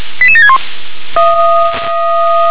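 Five-tone selective call (ZVEI), heard as quick runs of five tones, tones of about 70 ms. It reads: 88761
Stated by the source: ZVEI five-tone selective call received with an SDR.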